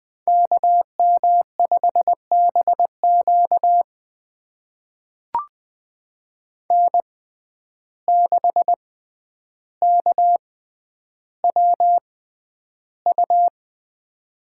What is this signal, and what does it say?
Morse code, heard as a keyed tone, about 20 wpm, spaced as for 10 wpm, 700 Hz. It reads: KM5BQ N6KWU
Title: USA Call Signs - 5 Characters - 6x Character Spacing - 20wpm